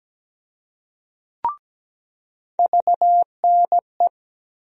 Morse code, heard as a keyed tone, about 17 wpm, 700 Hz. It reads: VNE
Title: USA Call Sign Postfixes - 3 Letters 17wpm